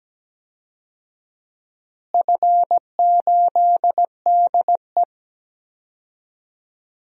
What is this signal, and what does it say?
Morse code, heard as a keyed tone, about 17 wpm, 700 Hz. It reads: F8DE
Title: DXCC-100 Call Signs - 4 Characters - Repeated 3x - 17wpm